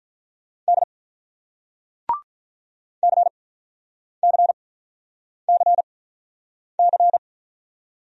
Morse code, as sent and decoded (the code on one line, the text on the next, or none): -.. / -.-. -.-. -.-. -.-.
D CCCC